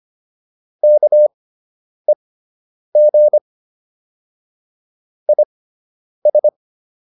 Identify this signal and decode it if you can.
Morse code, heard as a keyed tone, about 25 wpm, spaced as for 9 wpm, 600 Hz. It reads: KEG IS